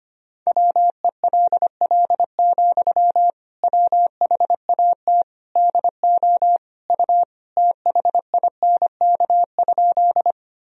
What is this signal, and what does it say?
Morse code, heard as a keyed tone, about 25 wpm, 700 Hz. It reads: WELL, WHAT DO U THINK?
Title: Sentences from Top 200 Words and Abbreviations 25wpm